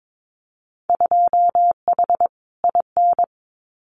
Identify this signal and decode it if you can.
Morse code, heard as a keyed tone, about 22 wpm, 700 Hz. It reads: 2H IN
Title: Winter Field Day Exchanges Encore 22wpm